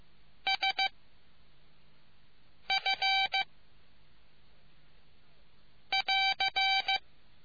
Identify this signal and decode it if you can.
Morse code, heard as a keyed tone, about 15 wpm, 770 Hz. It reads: S F <AR>